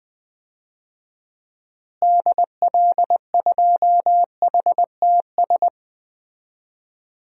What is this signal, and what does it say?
Morse code, heard as a keyed tone, about 20 wpm, 700 Hz. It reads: DL2HTS
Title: DXCC-100 Call Signs - 6 Characters - Repeated 3x - 20wpm